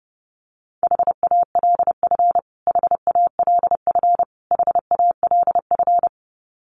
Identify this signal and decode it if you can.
Morse code, heard as a keyed tone, about 30 wpm, 700 Hz. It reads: HALF HALF HALF
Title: Top 500 Words Repeated 30wpm